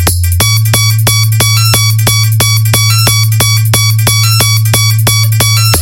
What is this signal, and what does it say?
There is no code.